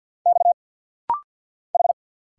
Morse code, heard as a keyed tone, about 50 wpm, 700 Hz. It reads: X H
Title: Single Letter-Number - Rapid-Fire 50wpm